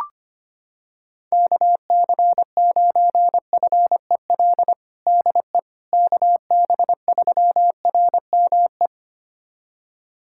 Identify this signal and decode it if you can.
Morse code, heard as a keyed tone, about 25 wpm, 700 Hz. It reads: KC9FEL DE KB3RME
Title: USA Callsign DE Callsign - 6 Characters - 25wpm